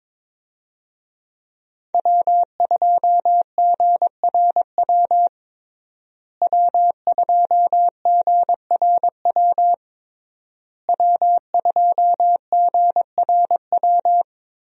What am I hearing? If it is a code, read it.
Morse code, heard as a keyed tone, about 22 wpm, 700 Hz. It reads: W2GRW W2GRW W2GRW
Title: USA Call Signs - 5 Characters - Repeated 3x - 22wpm